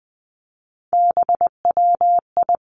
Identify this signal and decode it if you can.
Morse code, heard as a keyed tone, about 20 wpm, 700 Hz. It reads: BWI
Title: Three Challenging Letter-Number Combinations 20wpm